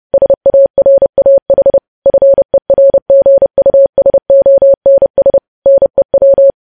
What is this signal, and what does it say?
Morse code, heard as a keyed tone, about 30 wpm, 550 Hz. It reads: SARAH FERGUSONS NEW
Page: News Headlines in Morse Code at 30 WPM